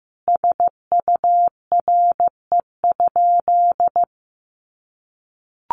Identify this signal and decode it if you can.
Morse code, heard as a keyed tone, about 15 wpm, 700 Hz. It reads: SURE?
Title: Sentences from Top 300 Words 15wpm